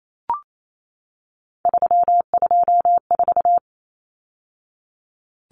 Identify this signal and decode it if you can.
Morse code, heard as a keyed tone, about 28 wpm, 700 Hz. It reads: 324
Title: Numbers - 3-Digit 28wpm